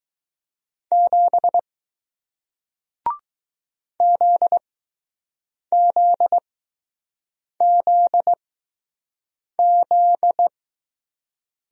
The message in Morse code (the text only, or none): 7 ZZZZ